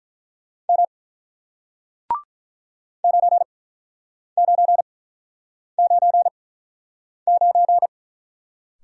Morse code, heard as a keyed tone, about 47 wpm, spaced as for 9 wpm, 700 Hz. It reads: M 9999